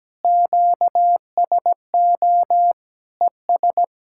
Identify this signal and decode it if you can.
Morse code, heard as a keyed tone, about 17 wpm, 700 Hz. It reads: QSO ES
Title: Practice QSO - Exchange 4 - Ending - 17wpm